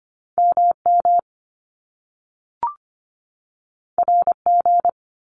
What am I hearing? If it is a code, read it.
Morse code, heard as a keyed tone, about 25 wpm, 700 Hz. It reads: MM RG
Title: Top Two Letter Combinations in Words 25wpm